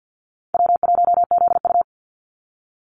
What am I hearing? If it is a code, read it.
Morse code, heard as a keyed tone, about 50 wpm, 700 Hz. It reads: W1ZU
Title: NA SOTA Activator Callsigns 50wpm